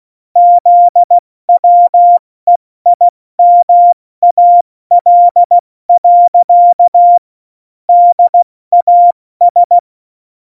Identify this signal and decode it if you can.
Morse code, heard as a keyed tone, about 16 wpm, 700 Hz. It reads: ZWEIMAL. DAS
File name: Morsekurs / Witze / Witz 05. Dezember 2017, 70 F80